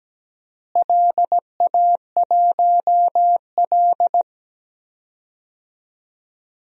Morse code, heard as a keyed tone, about 17 wpm, 700 Hz. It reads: LA1L
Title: DXCC-100 Call Signs - 4 Characters - Repeated 3x - 17wpm